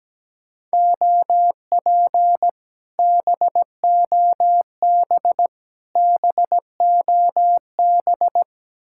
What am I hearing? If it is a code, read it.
Morse code, heard as a keyed tone, about 17 wpm, 700 Hz. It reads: OP BOB BOB
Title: Speed-Racing Common QSO Phrases - 17wpm